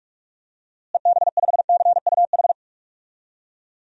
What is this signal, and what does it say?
Morse code, heard as a keyed tone, about 46 wpm, 700 Hz. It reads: EB5CUH
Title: DXCC-100 Call Signs - 6 Characters - Repeated 3x - 45wpm